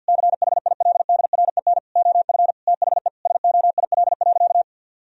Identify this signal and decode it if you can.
Morse code, heard as a keyed tone, about 50 wpm, 700 Hz. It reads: CHILDREN OF THE SOIL.
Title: Sentences from Top 1000 Words - Review Part 4 - 50wpm